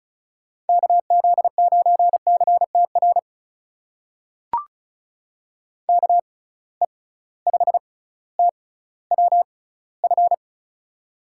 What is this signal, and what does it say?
Morse code, heard as a keyed tone, about 36 wpm, spaced as for 18 wpm, 700 Hz. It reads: KZ9CTR KE5TWF
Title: USA Call Signs - 6 Characters - 6x Character Spacing - 35wpm